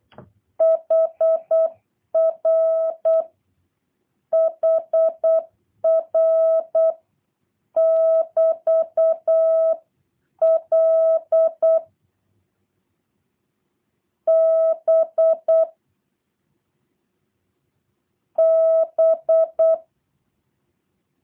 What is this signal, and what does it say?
Morse code, heard as a keyed tone, about 8 wpm, 650 Hz. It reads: HR HR <BT> L B B
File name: jh8cbh / 2023cw / review1-6